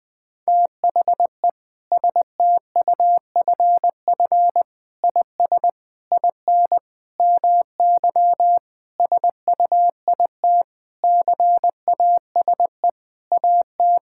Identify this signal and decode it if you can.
Morse code, heard as a keyed tone, about 20 wpm, 700 Hz. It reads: THE STUFF IS IN MY SUIT CASE AT
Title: Sentences from Top 1600 Words 20wpm